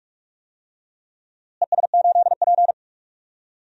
Morse code, heard as a keyed tone, about 46 wpm, 700 Hz. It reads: ES8P
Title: DXCC-100 Call Signs - 4 Characters - Repeated 2x - 45wpm